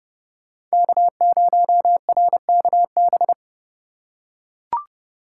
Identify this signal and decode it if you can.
Morse code, heard as a keyed tone, about 28 wpm, 700 Hz. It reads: K0RKB